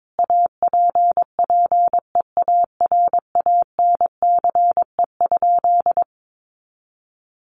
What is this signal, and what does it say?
Morse code, heard as a keyed tone, about 22 wpm, 700 Hz. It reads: APPEARANCE?
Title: Sentences from Top 1100 Words 22wpm